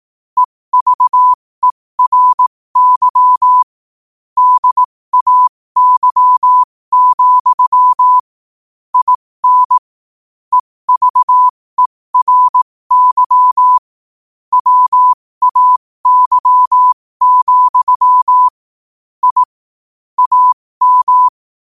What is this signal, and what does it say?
Morse code, heard as a keyed tone, about 18 wpm, spaced as for 15 wpm, 1000 Hz. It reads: EVERY DAY, IN EVERY WAY, I AM
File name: qotdfiles / 15wpm